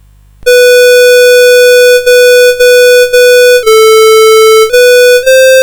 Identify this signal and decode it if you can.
There is no code.